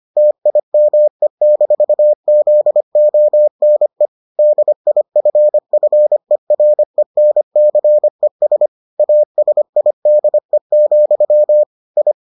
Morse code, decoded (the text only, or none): TIME-ZONE DIFFERENCES ASIDE, I